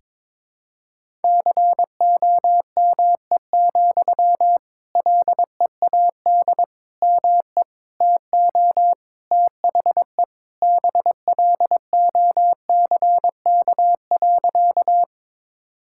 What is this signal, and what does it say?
Morse code, heard as a keyed tone, about 22 wpm, 700 Hz. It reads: COME, LEAD ME TO THE BLOCK.
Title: Sentences from Top 1600 Words 22wpm